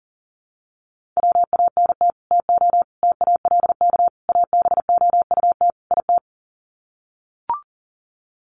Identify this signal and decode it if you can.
Morse code, heard as a keyed tone, about 40 wpm, 700 Hz. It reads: WANT TO TALK ABOUT IT
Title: Sets of 5 Words Encore 40wpm